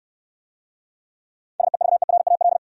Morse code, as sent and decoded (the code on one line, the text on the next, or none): ... ...- . .-. .- .-..
SEVERAL